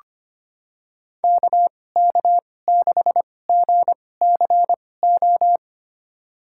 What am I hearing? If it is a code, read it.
Morse code, heard as a keyed tone, about 25 wpm, spaced as for 18 wpm, 700 Hz. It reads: KK6GCO
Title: USA Call Signs - 6 Characters - 2x Character Spacing - 25wpm